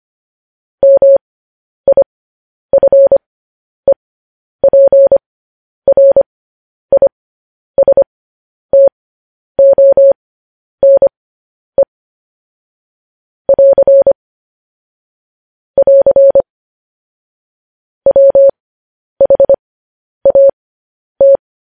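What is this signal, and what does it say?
Morse code, heard as a keyed tone, about 25 wpm, spaced as for 10 wpm, 550 Hz. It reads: MIFEPRISTONE <AR> <AR> WHAT
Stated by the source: Morse code news headlines practice at 10 wpm.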